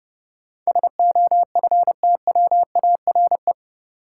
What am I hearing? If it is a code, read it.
Morse code, heard as a keyed tone, about 30 wpm, 700 Hz. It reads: SOFTWARE
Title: Sentences from Top 1000 Words - Review Part 3 - New Word Drill - 30wpm